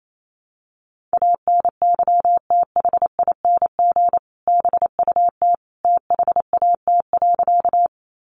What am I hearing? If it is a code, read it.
Morse code, heard as a keyed tone, about 28 wpm, 700 Hz. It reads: ANYTHING BUT THAT.
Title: Sentences from Top 1000 Words 28wpm